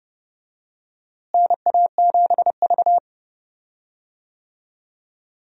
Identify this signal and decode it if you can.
Morse code, heard as a keyed tone, about 31 wpm, 700 Hz. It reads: NA7V
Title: USA Call Signs - 4 Characters - Repeated 2x - 30wpm